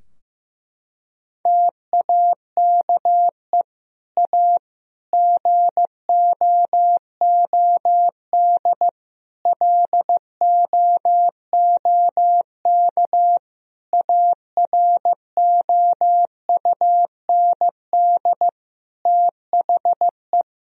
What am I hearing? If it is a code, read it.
Morse code, heard as a keyed tone, about 15 wpm, 700 Hz. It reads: TAKE A GOOD LOOK AROUND THE
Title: Sentences from Top 1000 Words - Review Part 3 - 15wpm